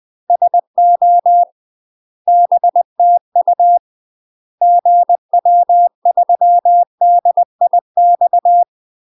Morse code, heard as a keyed tone, about 20 wpm, 700 Hz. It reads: SO BTU GW3DIX
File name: UK MASTER QSOs TRAINING / 20WPM / UK MASTER QSOs 0026